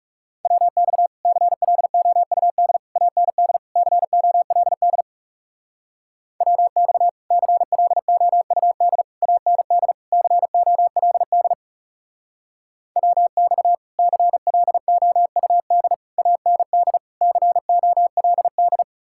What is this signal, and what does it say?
Morse code, heard as a keyed tone, about 36 wpm, 700 Hz. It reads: WX CLOUD AND COLD WX CLOUD AND COLD WX CLOUD AND COLD
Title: Speed-Racing Common QSO Phrases Encore - 30wpm